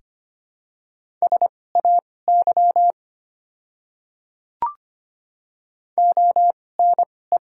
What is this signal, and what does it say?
Morse code, heard as a keyed tone, about 25 wpm, spaced as for 18 wpm, 690 Hz. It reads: SAY ONE
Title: 2x Character Spacing - Top 300 Words - 25wpm